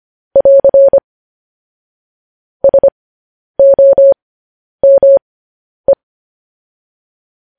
Morse code, heard as a keyed tone, about 25 wpm, spaced as for 10 wpm, 550 Hz. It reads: <AR> SOME
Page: news headlines in morse code at 10 wpm